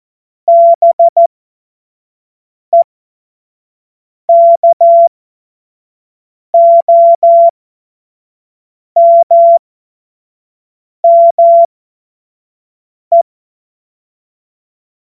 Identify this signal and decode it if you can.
Morse code, heard as a keyed tone, about 14 wpm, spaced as for 5 wpm, 680 Hz. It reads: BEKOMME
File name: Morsekurs / Witze / Witz 18. Juli 2023 25 F70